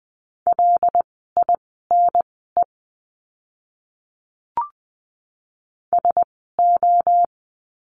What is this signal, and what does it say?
Morse code, heard as a keyed tone, about 20 wpm, spaced as for 14 wpm, 700 Hz. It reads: LINE SO
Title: 115 Lesson 19f - Letter w - Words 20wpm